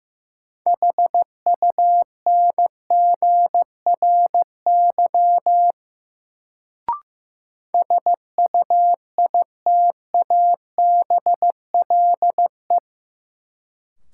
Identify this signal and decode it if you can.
Morse code, heard as a keyed tone, about 15 wpm, 700 Hz. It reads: HUNGRY SUITABLE
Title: Sentences from Top 1400 Words - New Word Drill Alt - 15wpm